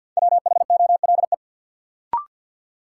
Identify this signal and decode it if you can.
Morse code, heard as a keyed tone, about 51 wpm, 700 Hz. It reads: WHOLE
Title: Sentences from Top 400 Words - New Word Drill - 50wpm